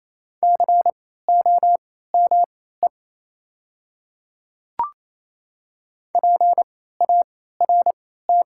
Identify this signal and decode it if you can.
Morse code, heard as a keyed tone, about 28 wpm, spaced as for 16 wpm, 700 Hz. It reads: COME PART
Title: 3x Character Spacing - Top 200 Words - 28wpm